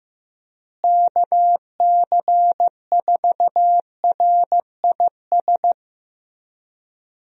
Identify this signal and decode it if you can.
Morse code, heard as a keyed tone, about 16 wpm, 700 Hz. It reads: KC4RIS
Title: USA Call Signs - 6 Characters - Repeated 3x - 15wpm